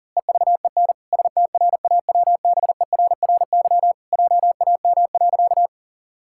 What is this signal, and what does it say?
Morse code, heard as a keyed tone, about 37 wpm, 700 Hz. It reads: EVEN STRAWBERRY JAM.
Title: Pangram Challenge v2 - 40wpm